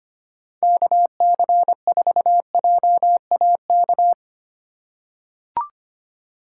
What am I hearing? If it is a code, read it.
Morse code, heard as a keyed tone, about 25 wpm, 700 Hz. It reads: KC4JAK
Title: USA Call Signs - 6 Characters - 4x Character Spacing - 25wpm